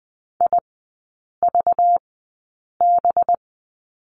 Morse code, heard as a keyed tone, about 20 wpm, spaced as for 8 wpm, 690 Hz. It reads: IVB
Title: 191 Lesson 31e - Letter v - Three Characters 20wpm